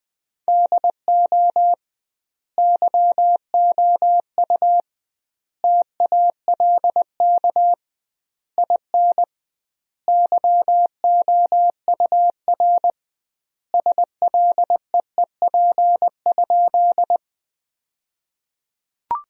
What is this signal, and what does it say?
Morse code, heard as a keyed tone, about 20 wpm, 700 Hz. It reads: DO YOU TALK IN YOUR SLEEP?
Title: 2x Word-Spacing - Sentences from Top 500 Words - 20wpm